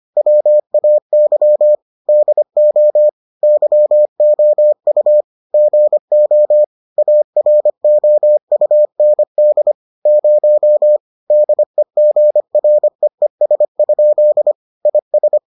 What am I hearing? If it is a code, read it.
Morse code, heard as a keyed tone, about 25 wpm, 600 Hz. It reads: WAY DO YOU GO AROUND 0 DEGREES? IS